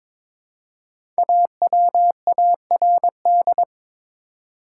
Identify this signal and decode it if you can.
Morse code, heard as a keyed tone, about 22 wpm, 700 Hz. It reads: AWARD
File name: 5 Letter Words - First 3 Letters Given - 22wpm